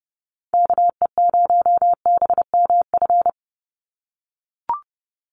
Morse code, heard as a keyed tone, about 30 wpm, 700 Hz. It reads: KE0BMF
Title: USA Call Signs 30wpm